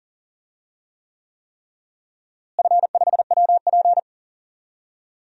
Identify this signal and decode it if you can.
Morse code, heard as a keyed tone, about 40 wpm, 700 Hz. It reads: F5WP
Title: DXCC-100 Call Signs - 4 Characters - Repeated 3x - 40wpm